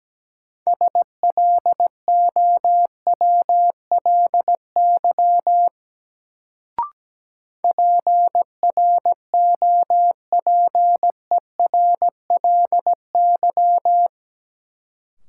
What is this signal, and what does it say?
Morse code, heard as a keyed tone, about 17 wpm, 700 Hz. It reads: SLOWLY PROPERLY
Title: Sentences from Top 1100 Words - New Word Drill - 17wpm